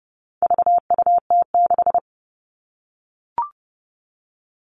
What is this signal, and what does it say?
Morse code, heard as a keyed tone, about 30 wpm, 700 Hz. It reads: VUT6
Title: All Cut Numbers - 4-Digit 30wpm